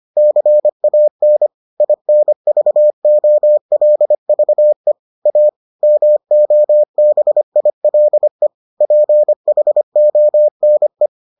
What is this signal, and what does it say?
Morse code, heard as a keyed tone, about 25 wpm, 600 Hz. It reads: CAN INVOLVE A MOBILE PHONE